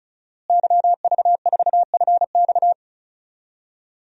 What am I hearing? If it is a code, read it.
Morse code, heard as a keyed tone, about 36 wpm, 700 Hz. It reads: YV4FX
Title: DXCC-100 Call Signs - 5 Characters - Repeated 3x - 35wpm